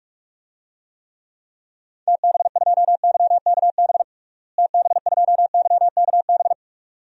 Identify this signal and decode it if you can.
Morse code, heard as a keyed tone, about 43 wpm, 700 Hz. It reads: TB2YKB TB2YKB
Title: DXCC-100 Call Signs - 6 Characters - Repeated 2x - 45wpm